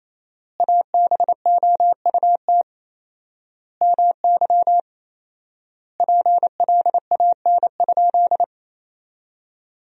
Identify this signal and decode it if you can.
Morse code, heard as a keyed tone, about 28 wpm, 700 Hz. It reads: ABOUT MY PLAN?
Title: 4x Word-Spacing - Sentences from Top 500 Words - 28wpm